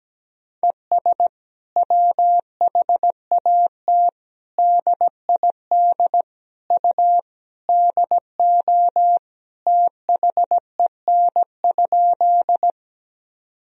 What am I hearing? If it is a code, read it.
Morse code, heard as a keyed tone, about 17 wpm, 700 Hz. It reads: ES WHAT DID U DO THEN?
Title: Sentences from Top 100 Words and Abbreviations 17wpm